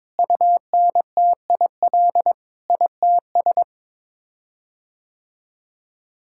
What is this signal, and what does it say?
Morse code, heard as a keyed tone, about 21 wpm, 700 Hz. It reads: UNTIL ITS
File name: practice / 22wpm / Sets of 8 Words - 22wpm